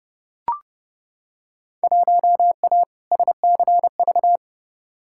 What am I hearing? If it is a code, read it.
Morse code, heard as a keyed tone, about 32 wpm, 700 Hz. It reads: 1A SCV
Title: Field Day Exchanges - 30wpm